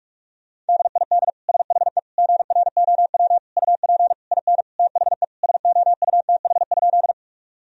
Morse code, heard as a keyed tone, about 46 wpm, 700 Hz. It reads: DID SHE GROW UP IN THE SOUTH?
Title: Sentences from Top 1000 Words - Review Part 2 - 45wpm